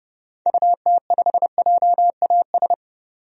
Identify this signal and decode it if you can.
Morse code, heard as a keyed tone, about 32 wpm, 700 Hz. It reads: UT5JAS
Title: DXCC-100 Call Signs - 6 Characters - 30wpm